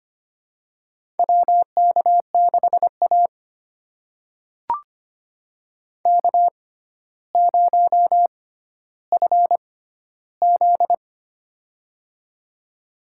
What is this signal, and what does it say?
Morse code, heard as a keyed tone, about 25 wpm, spaced as for 13 wpm, 700 Hz. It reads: WK6A K0FZ
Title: USA Call Signs - 4 Characters - 6x Character Spacing - 25wpm